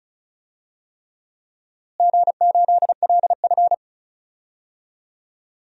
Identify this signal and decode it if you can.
Morse code, heard as a keyed tone, about 36 wpm, 700 Hz. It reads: G8LF
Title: DXCC-100 Call Signs - 4 Characters - Repeated 3x - 35wpm